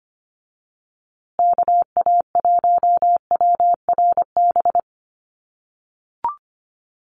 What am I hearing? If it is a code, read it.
Morse code, heard as a keyed tone, about 25 wpm, 700 Hz. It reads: KA1WRB
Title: USA Call Signs Repeated 3x 25wpm